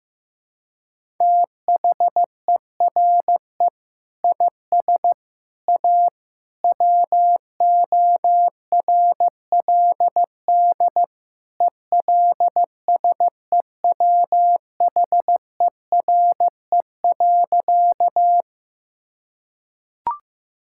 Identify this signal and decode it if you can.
Morse code, heard as a keyed tone, about 15 wpm, 700 Hz. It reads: THERE IS A WORLD ELSEWHERE.
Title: Sentences from Top 1600 Words 15wpm